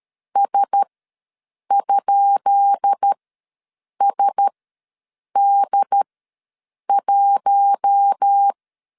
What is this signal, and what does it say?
Morse code, heard as a keyed tone, about 13 wpm, spaced as for 7 wpm, 800 Hz. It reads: S?SD1
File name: LEZIONI / 10 SIGNS / 10B INT ws 12wpm 3527